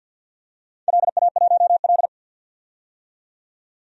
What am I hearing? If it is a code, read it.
Morse code, heard as a keyed tone, about 53 wpm, 700 Hz. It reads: LA1L
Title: DXCC-100 Call Signs - 4 Characters - 50wpm